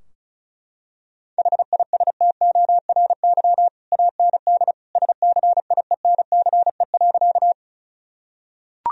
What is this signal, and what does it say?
Morse code, heard as a keyed tone, about 36 wpm, 700 Hz. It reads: HISTORY AND SCIENCE.
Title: Sentences from Top 1000 Words - Review Part 2 - 35wpm